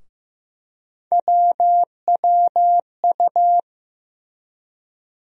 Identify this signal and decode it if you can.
Morse code, heard as a keyed tone, about 15 wpm, 700 Hz. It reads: WWU
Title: USA Call Sign Postfixes - 3 Letters 15wpm